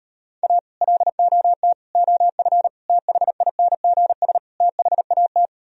Morse code, heard as a keyed tone, about 39 wpm, 700 Hz. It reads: A LOT OF THINGS THAT